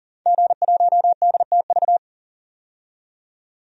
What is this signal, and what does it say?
Morse code, heard as a keyed tone, about 40 wpm, 700 Hz. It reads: G1DTV
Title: DXCC-100 Call Signs - 5 Characters - 40wpm